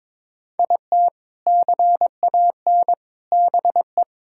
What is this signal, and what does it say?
Morse code, heard as a keyed tone, about 21 wpm, 700 Hz. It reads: IT CAN BE
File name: practice / 22wpm / Sentences with 2-Word Phrases - 22wpm